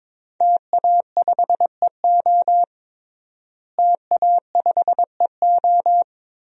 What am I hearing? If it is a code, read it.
Morse code, heard as a keyed tone, about 23 wpm, 700 Hz. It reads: TA5EO TA5EO
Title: DXCC-100 Call Signs - 5 Characters - Repeated 2x - 22wpm